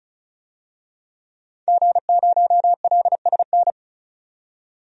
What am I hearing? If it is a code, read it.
Morse code, heard as a keyed tone, about 36 wpm, 700 Hz. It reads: G0LSN
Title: DXCC-100 Call Signs - 5 Characters - 35wpm